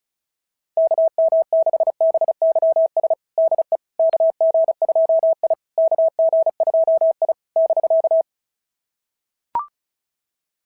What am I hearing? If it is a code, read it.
Morse code, heard as a keyed tone, about 34 wpm, 660 Hz. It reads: KM6BYS DE KG2I KG2I <BK>
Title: SOTA QSO Practice 35wpm